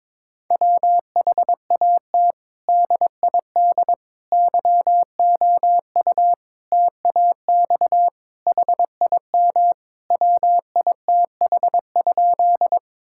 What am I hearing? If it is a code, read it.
Morse code, heard as a keyed tone, about 22 wpm, 700 Hz. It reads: WHAT DID YOU TAX HIM WITH?
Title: Sentences from Top 1000 Words - Review Part 5 - 22wpm